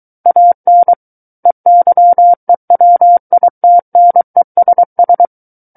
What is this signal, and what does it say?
Morse code, heard as a keyed tone, about 23 wpm, 700 Hz. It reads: AN EYEWITNESS
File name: Book One / 14-Chapter 14 - 23 WPM